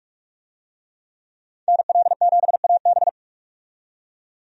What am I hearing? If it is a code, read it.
Morse code, heard as a keyed tone, about 47 wpm, 700 Hz. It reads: NL7AB
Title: USA Call Signs - 5 Characters - 45wpm